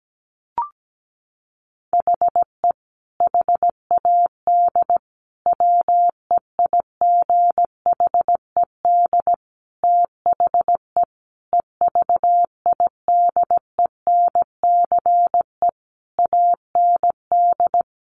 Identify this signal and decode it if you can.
Morse code, heard as a keyed tone, about 17 wpm, 700 Hz. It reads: HE HAD WEIGHED THE EVIDENCE AND
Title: A Sound Of Thunder 17wpm